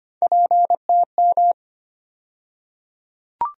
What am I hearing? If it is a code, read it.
Morse code, heard as a keyed tone, about 25 wpm, 700 Hz. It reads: PTM
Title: USA Call Sign Postfixes - 3 Letters 25wpm